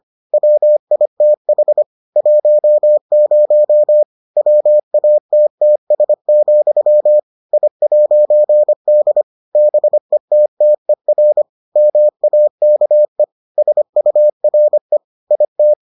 Morse code, heard as a keyed tone, about 25 wpm, 600 Hz. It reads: WITH 10 WATTS, I'D BETTER MAKE SURE IT